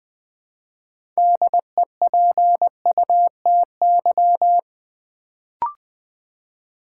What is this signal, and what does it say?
Morse code, heard as a keyed tone, about 20 wpm, 700 Hz. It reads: DEPUTY
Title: Build A Word - 2 Letters At A Time - 6-Letter Words - 20wpm